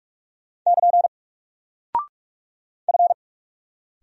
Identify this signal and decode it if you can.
Morse code, heard as a keyed tone, about 44 wpm, 700 Hz. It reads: <KN> F
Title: Single Character (Letter:Number:Punctuation:Prosign) Alt 45wpm